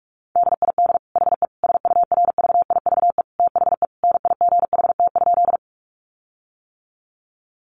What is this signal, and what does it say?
Morse code, heard as a keyed tone, about 44 wpm, 700 Hz. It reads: DID HE SURVIVE THE NIGHT?
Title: Sentences from Top 1300 Words 45wpm